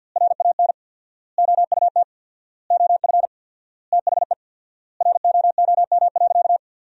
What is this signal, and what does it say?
Morse code, heard as a keyed tone, about 53 wpm, 700 Hz. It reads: RAN OUT OF THE ROOM.
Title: 4x Word-Spacing - Sentences from Top 500 Words - 50wpm